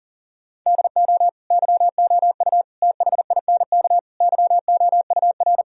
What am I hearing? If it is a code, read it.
Morse code, heard as a keyed tone, about 40 wpm, 700 Hz. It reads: DO YOU THINK YOUR
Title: Sentences from Top 1000 Words - Review Part 1 - 40wpm